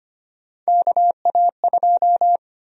Morse code, heard as a keyed tone, about 25 wpm, 700 Hz. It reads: KA2
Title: USA Call Sign Prefixes - 25wpm